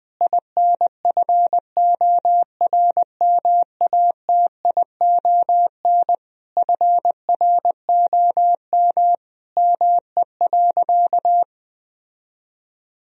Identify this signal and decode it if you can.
Morse code, heard as a keyed tone, about 21 wpm, 700 Hz. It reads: INFORMATION FROM ME.
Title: Sentences from Top 1000 Words - Review Part 3 - 20wpm